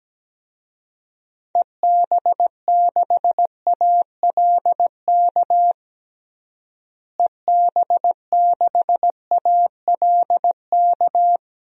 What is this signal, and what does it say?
Morse code, heard as a keyed tone, about 17 wpm, 700 Hz. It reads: EB6ALK EB6ALK
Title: DXCC-100 Call Signs - 6 Characters - Repeated 3x - 17wpm